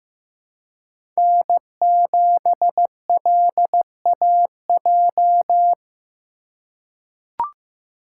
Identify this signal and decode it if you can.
Morse code, heard as a keyed tone, about 15 wpm, 700 Hz. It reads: N7LAJ
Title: USA Call Signs - 5 Characters - 2x Character Spacing - 15wpm